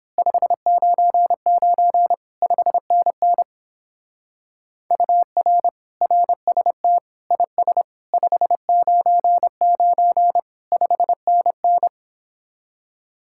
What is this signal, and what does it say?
Morse code, heard as a keyed tone, about 26 wpm, 700 Hz. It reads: 599 5NN UR RST IS 599 5NN
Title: Speed-Racing Common QSO Phrases - 22wpm